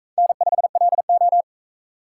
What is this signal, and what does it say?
Morse code, heard as a keyed tone, about 41 wpm, 700 Hz. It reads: N5LO